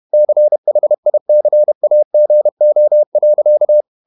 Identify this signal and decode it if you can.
Morse code, heard as a keyed tone, about 31 wpm, 600 Hz. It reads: CHICAGO.
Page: Never 31wpm | CW med Gnister